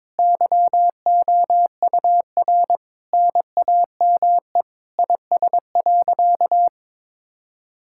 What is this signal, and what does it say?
Morse code, heard as a keyed tone, about 22 wpm, 700 Hz. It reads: YOUR NAME IS.
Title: Sentences from Top 1000 Words - Review Part 6 - 22wpm